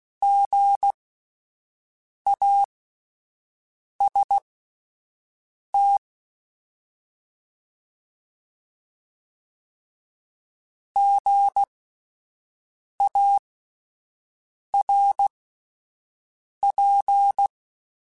Morse code, heard as a keyed tone, about 16 wpm, spaced as for 5 wpm, 800 Hz. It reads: GAST GARP
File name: Spaced Alphas / Spaced Alphas mp3s / Gs sp @10